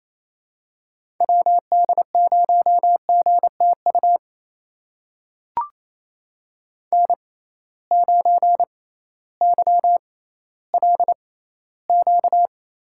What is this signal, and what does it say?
Morse code, heard as a keyed tone, about 28 wpm, spaced as for 15 wpm, 700 Hz. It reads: WD0GTU N9YLQ
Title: USA Call Signs - 6x Character Spacing - 28wpm